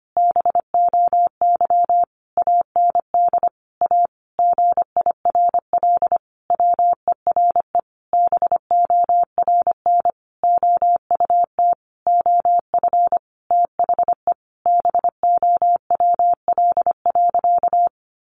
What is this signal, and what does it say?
Morse code, heard as a keyed tone, about 25 wpm, 700 Hz. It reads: BOY AND A GIRL WERE BORN OUT OF THE BOWL.
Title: Sentences from Top 1200 Words 25wpm